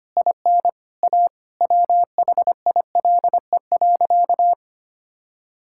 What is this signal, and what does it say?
Morse code, heard as a keyed tone, about 25 wpm, 700 Hz. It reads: IN A WHILE.
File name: practice / 25wpm / Sentences with 4-Word Phrases - 25wpm